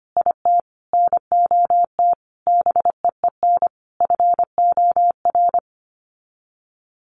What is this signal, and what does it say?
Morse code, heard as a keyed tone, about 25 wpm, 700 Hz. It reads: IT NOT BEEN FOR
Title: Sets of 5 Words Encore2 25wpm